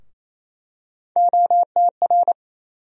Morse code, heard as a keyed tone, about 28 wpm, 700 Hz. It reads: OTR
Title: USA Call Sign Postfixes - 3 Letters 28wpm